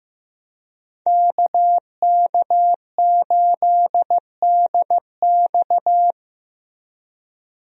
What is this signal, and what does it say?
Morse code, heard as a keyed tone, about 15 wpm, 700 Hz. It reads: KK8DX
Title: USA Call Signs - 5 Characters - Repeated 2x - 15wpm